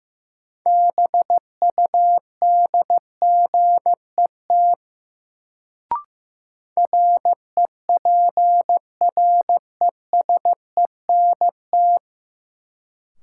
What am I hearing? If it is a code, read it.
Morse code, heard as a keyed tone, about 15 wpm, 700 Hz. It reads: BUDGET REPRESENT
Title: Sentences from Top 1200 Words - New Word Drill - 15wpm